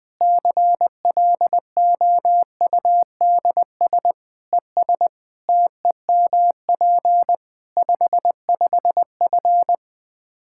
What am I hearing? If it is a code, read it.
Morse code, heard as a keyed tone, about 20 wpm, 700 Hz. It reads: CLOUDS ES TEMP 55F
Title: K1USN SST QSO Practice 20wpm